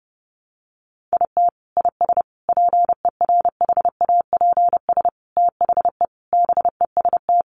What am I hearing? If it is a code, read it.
Morse code, heard as a keyed tone, about 30 wpm, 700 Hz. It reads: IT IS PERHAPS THE BEST
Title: Sentences from Top 1300 Words 30wpm